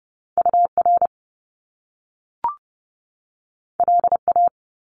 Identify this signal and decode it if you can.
Morse code, heard as a keyed tone, about 30 wpm, 700 Hz. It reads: UR LA
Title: Top Two Letter Combinations in Words 30wpm